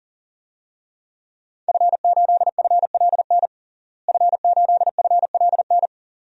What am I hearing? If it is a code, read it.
Morse code, heard as a keyed tone, about 40 wpm, 700 Hz. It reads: F8FLN F8FLN
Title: DXCC-100 Call Signs - 5 Characters - Repeated 2x - 40wpm